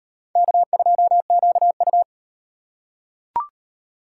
Morse code, heard as a keyed tone, about 38 wpm, 700 Hz. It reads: K2QU